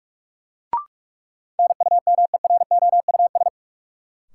Morse code, heard as a keyed tone, about 46 wpm, 700 Hz. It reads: NUMEROUS
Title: Sentences from Top 1400 Words - New Word Drill - 45wpm